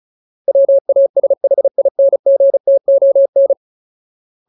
Morse code, heard as a keyed tone, about 34 wpm, 550 Hz. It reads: WASHINGTON
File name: Ultimate Word List / 35WPM Ultimate